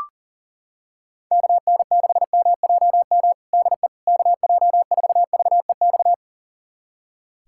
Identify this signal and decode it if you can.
Morse code, heard as a keyed tone, about 40 wpm, 690 Hz. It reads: KN6MJM DE KJ4VEX
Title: USA Callsign DE Callsign - 6 Characters - 40wpm